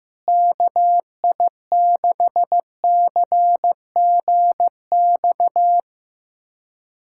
Morse code, heard as a keyed tone, about 15 wpm, 700 Hz. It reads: KI6CGX
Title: USA Callsign DE Callsign 15wpm